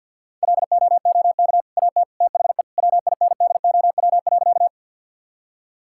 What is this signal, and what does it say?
Morse code, heard as a keyed tone, about 50 wpm, 700 Hz. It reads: LOOK AT THE WINDOW.
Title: Sentences from Top 1600 Words 50wpm